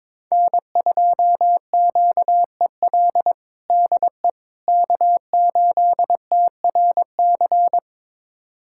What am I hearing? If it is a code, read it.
Morse code, heard as a keyed tone, about 21 wpm, 700 Hz. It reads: N2QEL DE K8TRC
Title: USA Callsign DE Callsign - 5 Characters - 22wpm